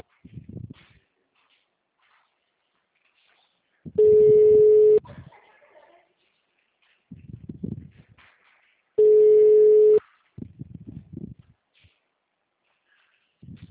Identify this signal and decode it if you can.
Ringback tone (Europe, 425 Hz).